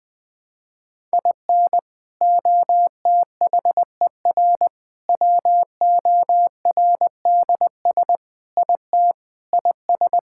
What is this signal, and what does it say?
Morse code, heard as a keyed tone, about 19 wpm, 700 Hz. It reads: IN OTHER WORDS IT IS
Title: Sentences with 3-Word Phrases - Spoken Beginning - 20wpm